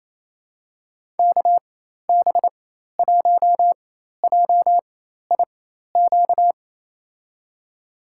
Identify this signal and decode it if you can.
Morse code, heard as a keyed tone, about 28 wpm, spaced as for 13 wpm, 700 Hz. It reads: KB1JIQ
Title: USA Call Signs - 4x Character Spacing - 28wpm